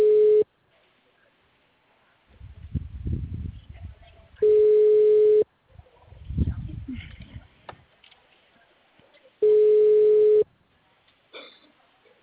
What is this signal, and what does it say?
Ringback tone (Europe, 425 Hz).